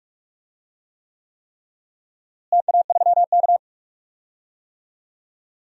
Morse code, heard as a keyed tone, about 46 wpm, 700 Hz. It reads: TA3K
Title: DXCC-100 Call Signs - 4 Characters - Repeated 3x - 45wpm